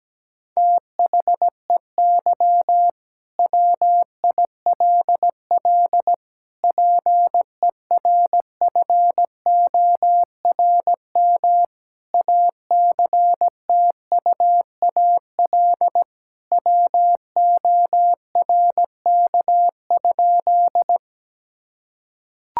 Morse code, heard as a keyed tone, about 17 wpm, 700 Hz. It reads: THEY WILL PERFORM ACTUAL WORK?
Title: Sentences from Top 1000 Words - Review Part 5 - 17wpm